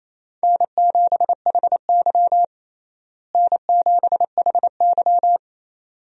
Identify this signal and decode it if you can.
Morse code, heard as a keyed tone, about 28 wpm, 700 Hz. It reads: N7HY N7HY
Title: USA Call Signs - 4 Characters - Repeated 2x - 28wpm